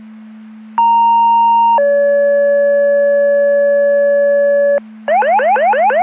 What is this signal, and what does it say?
Two-tone page, heard as a one-second tone, then a three-second tone, about 930 then 570 Hz, with noise. It is heard from 0.8 to 4.8 s.